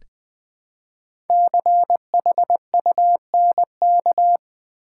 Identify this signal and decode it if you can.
Morse code, heard as a keyed tone, about 20 wpm, 700 Hz. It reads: CHUNK
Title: Build A Word - 1 Letter At A Time - 5-Letter Words - 20wpm